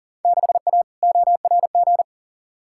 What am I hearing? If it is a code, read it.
Morse code, heard as a keyed tone, about 40 wpm, 700 Hz. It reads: 6A ORG
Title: Field Day Exchanges Encore - 40wpm